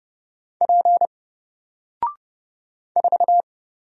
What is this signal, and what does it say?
Morse code, heard as a keyed tone, about 29 wpm, 700 Hz. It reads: P 4
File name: Single-Letter-Number-30wpm